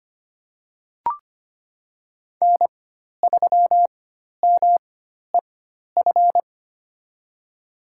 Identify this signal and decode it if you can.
Morse code, heard as a keyed tone, about 25 wpm, spaced as for 12 wpm, 700 Hz. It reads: N3MEF
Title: USA Call Signs - 5 Characters - 4x Character Spacing - 25wpm